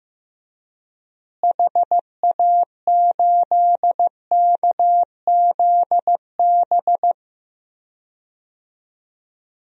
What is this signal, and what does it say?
Morse code, heard as a keyed tone, about 15 wpm, 700 Hz. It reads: HA8KZB
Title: DXCC-100 Call Signs - 6 Characters - Repeated 3x - 15wpm